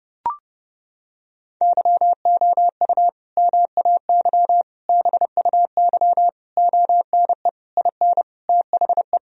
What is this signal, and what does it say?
Morse code, heard as a keyed tone, about 29 wpm, 700 Hz. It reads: YOU MAY BUY ONE IN THE
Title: Sentences from Top 1000 Words - Review Part 4 - 30wpm